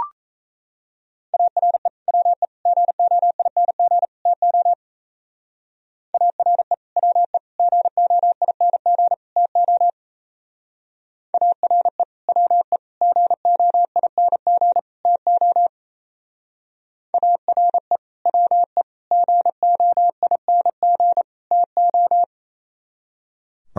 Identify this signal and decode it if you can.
Morse code, heard as a keyed tone, about 34 wpm, 700 Hz. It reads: ARE WE GOING TO ARE WE GOING TO ARE WE GOING TO ARE WE GOING TO
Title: Speed-Racing - Sets of 4 Words 28wpm